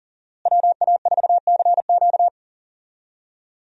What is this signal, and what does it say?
Morse code, heard as a keyed tone, about 40 wpm, 700 Hz. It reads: WA4CQ
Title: USA Call Signs - 5 Characters - Repeated 2x - 40wpm